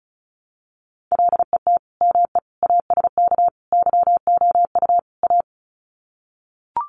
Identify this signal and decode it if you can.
Morse code, heard as a keyed tone, about 34 wpm, 700 Hz. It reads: LET ME ASK YOU A
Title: Sets of 5 Words Encore 35wpm